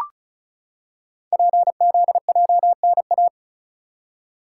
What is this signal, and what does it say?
Morse code, heard as a keyed tone, about 36 wpm, 700 Hz. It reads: PZJNA
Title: Code Groups - 5 Letters - 35wpm